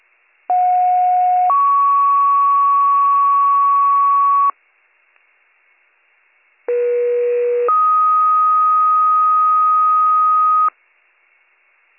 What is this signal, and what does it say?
Two-tone page, heard as a one-second tone, then a three-second tone, about 730 then 1120 Hz, about 480 then 1250 Hz. It is heard from 0.5 to 10.7 s.